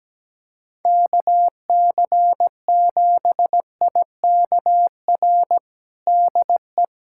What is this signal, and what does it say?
Morse code, heard as a keyed tone, about 17 wpm, 700 Hz. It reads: KC7IKR DE
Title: USA Callsign DE Callsign 17wpm